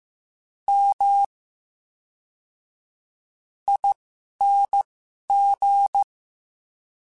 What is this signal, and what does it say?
Morse code, heard as a keyed tone, about 15 wpm, spaced as for 11 wpm, 800 Hz. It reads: M ING
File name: Special INGs / Special INGs-mp3s / VerySlowToFaster Ing @10